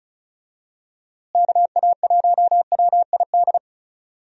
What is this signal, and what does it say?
Morse code, heard as a keyed tone, about 36 wpm, 700 Hz. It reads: KA1WID
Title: USA Call Signs - 6 Characters - Repeated 3x - 35wpm